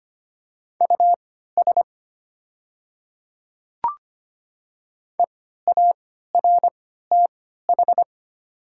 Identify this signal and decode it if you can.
Morse code, heard as a keyed tone, about 25 wpm, spaced as for 14 wpm, 690 Hz. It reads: US EARTH
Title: 3x Character Spacing - Top 300 Words - 25wpm